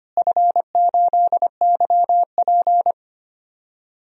Morse code, heard as a keyed tone, about 25 wpm, 700 Hz. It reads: F8YP
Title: DXCC-100 Call Signs - 4 Characters - Repeated 2x - 25wpm